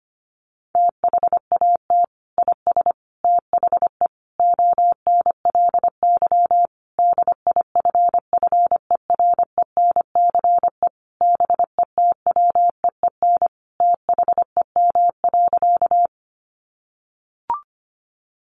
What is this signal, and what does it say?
Morse code, heard as a keyed tone, about 25 wpm, 700 Hz. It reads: THAT IS THE ONLY DIFFERENCE BETWEEN THEM.
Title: Sentences from Top 800 Words 25wpm